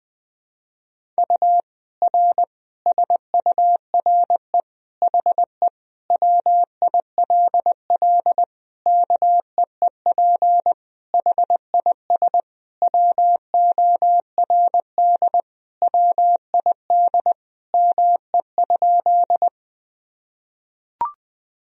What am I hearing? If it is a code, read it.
Morse code, heard as a keyed tone, about 20 wpm, 700 Hz. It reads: U R SURE HE WILL KEEP HIS WORD WID ME?
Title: Sentences from Top 300 Words and Abbreviations 20wpm